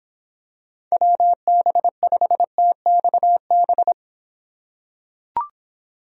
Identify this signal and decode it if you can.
Morse code, heard as a keyed tone, about 26 wpm, 700 Hz. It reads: WB5TXB